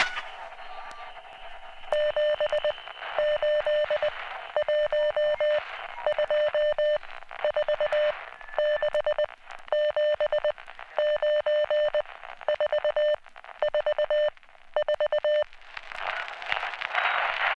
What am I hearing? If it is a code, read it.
Morse code, heard as a keyed tone, about 20 wpm, 600 Hz. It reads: 7 8 1 2 4 6 7 9 4 4 4